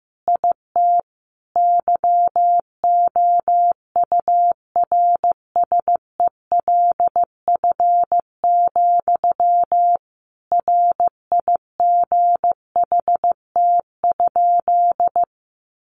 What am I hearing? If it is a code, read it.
Morse code, heard as a keyed tone, about 15 wpm, 700 Hz. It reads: IT YOURSELF, RIGHT?
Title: Sentences from Top 1200 Words 15wpm